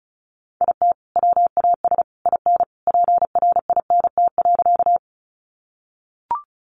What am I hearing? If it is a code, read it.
Morse code, heard as a keyed tone, about 34 wpm, 700 Hz. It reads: IT WAS IN PRINT.
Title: Sentences from Top 1400 Words 35wpm